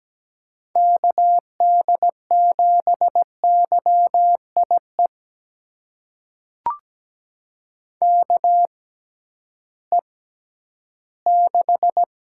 Morse code, heard as a keyed tone, about 17 wpm, spaced as for 11 wpm, 700 Hz. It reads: KD7YIE KE6
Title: USA Call Signs - 6 Characters - 6x Character Spacing - 17wpm